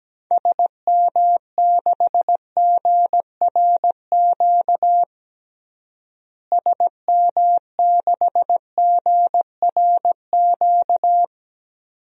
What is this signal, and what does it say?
Morse code, heard as a keyed tone, about 17 wpm, 700 Hz. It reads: SM6GRQ SM6GRQ
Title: DXCC-100 Call Signs - 6 Characters - Repeated 2x - 17wpm